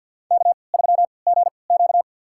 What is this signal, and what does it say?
Morse code, heard as a keyed tone, about 47 wpm, spaced as for 28 wpm, 700 Hz. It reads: K3GQ
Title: USA Call Signs - 4 Characters - 3x Character Spacing - 50wpm